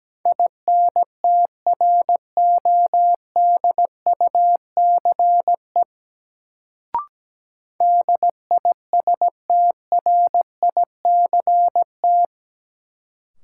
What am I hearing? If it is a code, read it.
Morse code, heard as a keyed tone, about 17 wpm, 700 Hz. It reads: INTRODUCE DISTRICT
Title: Sentences from Top 1600 Words - New Word Drill - 17wpm